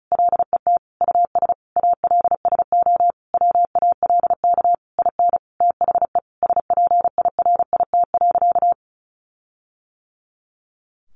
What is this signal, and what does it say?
Morse code, heard as a keyed tone, about 34 wpm, 700 Hz. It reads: LET US ALSO WALK IN THE SPIRIT.
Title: Sentences from Top 1400 Words 35wpm